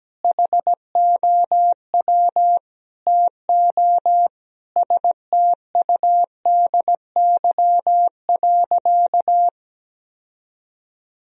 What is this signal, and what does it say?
Morse code, heard as a keyed tone, about 17 wpm, 700 Hz. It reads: HOW TO STUDY.
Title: Sentences from Top 1000 Words - Review Part 1 - 17wpm